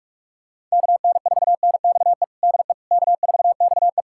Morse code, heard as a keyed tone, about 41 wpm, 700 Hz. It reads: KN4NXE DE K4XE
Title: USA Callsign DE Callsign 45wpm